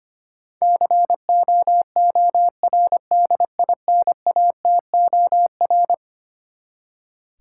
Morse code, heard as a keyed tone, about 25 wpm, 700 Hz. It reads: COORDINATOR
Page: morse-code-ninja-practice-25wpm